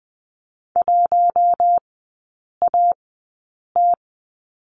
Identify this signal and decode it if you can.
Morse code, heard as a keyed tone, about 20 wpm, spaced as for 8 wpm, 700 Hz. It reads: 1AT
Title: 041 Lesson 8e - Number 1 - Three Characters 20wpm